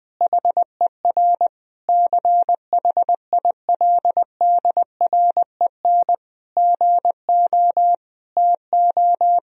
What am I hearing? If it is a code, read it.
Morse code, heard as a keyed tone, about 20 wpm, 700 Hz. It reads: HER CHILDREN GO TO
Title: Sentences from Top 1000 Words - Review Part 4 - 20wpm